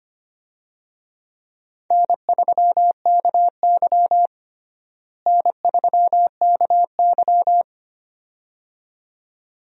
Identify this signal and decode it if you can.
Morse code, heard as a keyed tone, about 25 wpm, 700 Hz. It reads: N3KY N3KY
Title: USA Call Signs - 4 Characters - Repeated 2x - 25wpm